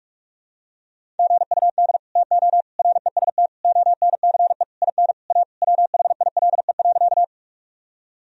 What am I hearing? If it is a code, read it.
Morse code, heard as a keyed tone, about 46 wpm, 700 Hz. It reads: GUD TO REST ONCE IN A WHILE.
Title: Sentences from Top 500 Words and Abbreviations 45wpm